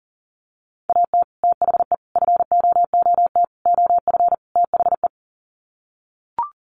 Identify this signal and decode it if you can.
Morse code, heard as a keyed tone, about 40 wpm, 700 Hz. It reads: AT THE FOOT OF THE
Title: Sets of 5 Words 40wpm